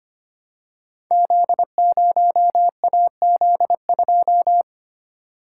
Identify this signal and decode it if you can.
Morse code, heard as a keyed tone, about 25 wpm, 700 Hz. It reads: Z0AZ2
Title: Code Groups - 5 Letter:Numbers - 25wpm